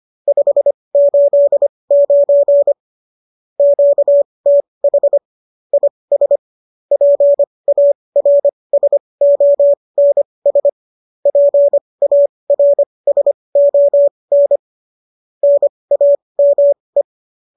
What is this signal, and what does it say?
Morse code, heard as a keyed tone, about 25 wpm, spaced as for 20 wpm, 570 Hz. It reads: 589 QTH IS PARSONS PARSON NAME
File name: qso205 20